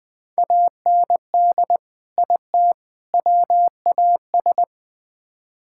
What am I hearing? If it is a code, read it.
Morse code, heard as a keyed tone, about 20 wpm, 700 Hz. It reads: AND IT WAS
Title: Build A Phrase - 3-Word Phrases - 20wpm